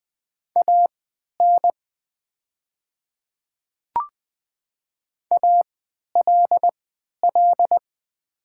Morse code, heard as a keyed tone, about 20 wpm, spaced as for 11 wpm, 700 Hz. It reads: AN ALL
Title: 3x Character Spacing - Top 100 Words - 20wpm